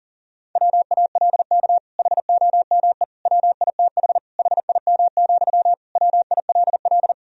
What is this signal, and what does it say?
Morse code, heard as a keyed tone, about 40 wpm, 700 Hz. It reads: WALK HOME WITH HIM, WILL
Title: Sentences from Top 1000 Words - Review Part 1 - 40wpm